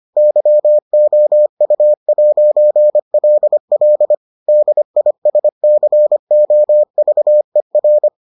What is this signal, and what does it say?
Morse code, heard as a keyed tone, about 25 wpm, 600 Hz. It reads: YOU'LL DISCOVER